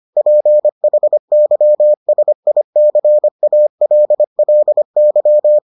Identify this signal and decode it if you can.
Morse code, heard as a keyed tone, about 25 wpm, 600 Hz. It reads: PHYSICALLY